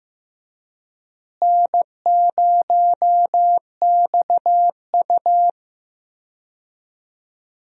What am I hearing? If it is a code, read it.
Morse code, heard as a keyed tone, about 15 wpm, 700 Hz. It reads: N0XU
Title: USA Call Signs - 4 Characters - Repeated 2x - 15wpm